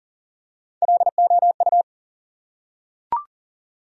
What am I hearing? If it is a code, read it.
Morse code, heard as a keyed tone, about 40 wpm, 700 Hz. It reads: LOU
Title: USA Common Names - Max 3 Letters - 40wpm